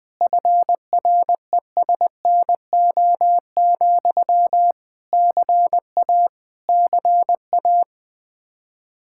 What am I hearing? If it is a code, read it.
Morse code, heard as a keyed tone, about 20 wpm, 700 Hz. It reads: FRESNO, CA CA
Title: Practice QSO - Exchange 1 - Beginning - 20wpm